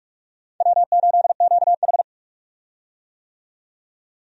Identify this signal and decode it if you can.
Morse code, heard as a keyed tone, about 46 wpm, 700 Hz. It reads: W8QH
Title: USA Call Signs - 4 Characters - 45wpm